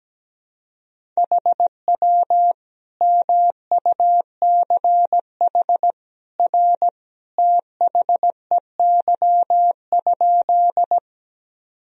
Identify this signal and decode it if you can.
Morse code, heard as a keyed tone, about 17 wpm, 700 Hz. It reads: HW MUCH R THEY?
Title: Sentences from Top 200 Words and Abbreviations 17wpm